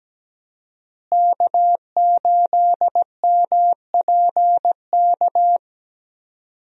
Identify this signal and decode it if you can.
Morse code, heard as a keyed tone, about 17 wpm, 700 Hz. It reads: K8MPK
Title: USA Call Signs - 5 Characters - Repeated 2x - 17wpm